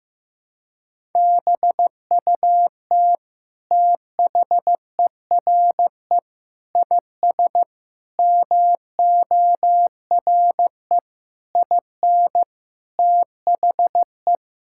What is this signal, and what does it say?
Morse code, heard as a keyed tone, about 16 wpm, 700 Hz. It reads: BUT THERE IS MORE IN THE
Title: Sentences from Top 1000 Words - Review Part 5 - 15wpm